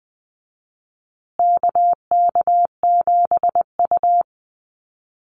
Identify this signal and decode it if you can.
Morse code, heard as a keyed tone, about 20 wpm, 700 Hz. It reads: KK7U
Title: USA Call Signs Repeated 2x 20wpm